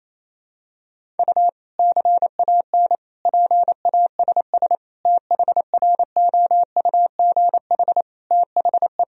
Morse code, heard as a keyed tone, about 28 wpm, 700 Hz. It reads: U CAN PASS THROUGH THE
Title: Sentences from Top 400 Words and Abbreviations 28wpm